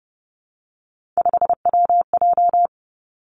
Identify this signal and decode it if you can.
Morse code, heard as a keyed tone, about 30 wpm, 700 Hz. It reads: 5WJ
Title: Three Challenging Letter-Number Combinations 30wpm